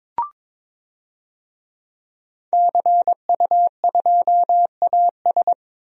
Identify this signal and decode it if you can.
Morse code, heard as a keyed tone, about 22 wpm, 700 Hz. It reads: CU2AS
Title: DXCC-100 Call Signs - 5 Characters - 22wpm